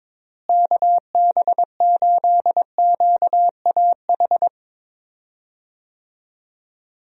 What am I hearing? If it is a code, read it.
Morse code, heard as a keyed tone, about 22 wpm, 700 Hz. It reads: KB8QAH
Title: USA Call Signs - 6 Characters - Repeated 2x - 22wpm